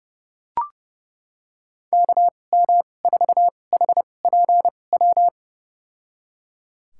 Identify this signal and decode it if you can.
Morse code, heard as a keyed tone, about 30 wpm, spaced as for 22 wpm, 700 Hz. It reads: KM4HPW
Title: USA Call Signs - 6 Characters - 2x Character Spacing - 30wpm